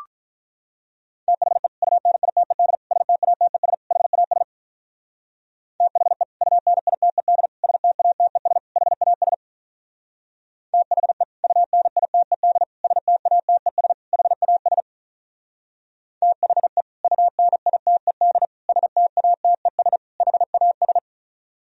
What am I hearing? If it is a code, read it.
Morse code, heard as a keyed tone, about 46 wpm, 700 Hz. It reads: THE UNITED STATES HAS THE UNITED STATES HAS THE UNITED STATES HAS THE UNITED STATES HAS